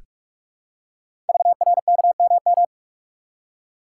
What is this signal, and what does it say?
Morse code, heard as a keyed tone, about 44 wpm, 700 Hz. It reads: VRKMM
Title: Code Groups - 5 Letters - 45wpm